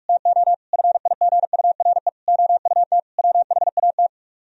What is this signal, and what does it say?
Morse code, heard as a keyed tone, about 45 wpm, 700 Hz. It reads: TO FIGURE OUT WHAT